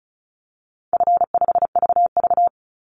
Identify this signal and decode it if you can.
Morse code, heard as a keyed tone, about 34 wpm, 700 Hz. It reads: F5VV
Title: Call Signs - 35wpm